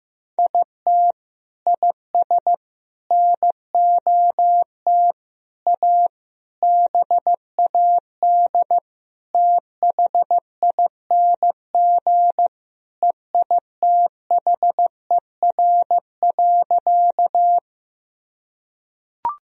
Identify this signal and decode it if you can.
Morse code, heard as a keyed tone, about 16 wpm, 700 Hz. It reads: IT IS NOT A BAD THING EITHER.
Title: Sentences from Top 1000 Words - Review Part 4 - 15wpm